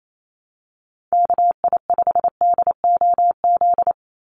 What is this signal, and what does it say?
Morse code, heard as a keyed tone, about 28 wpm, 700 Hz. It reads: KI5DOZ
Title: USA Call Signs 28wpm